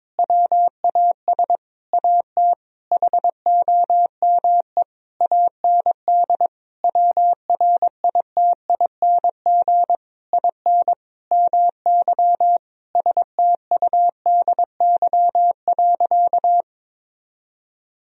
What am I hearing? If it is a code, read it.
Morse code, heard as a keyed tone, about 22 wpm, 700 Hz. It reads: WAS AT HOME AND WRITING IN MY STUDY.
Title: Sentences from Top 1000 Words - Review Part 1 - 22wpm